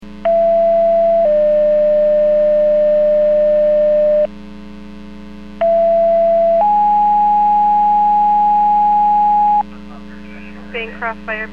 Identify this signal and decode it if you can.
Two-tone page, heard as a one-second tone, then a three-second tone, about 670 then 600 Hz, about 670 then 830 Hz, with speech. It is heard from 0.3 to 9.6 s.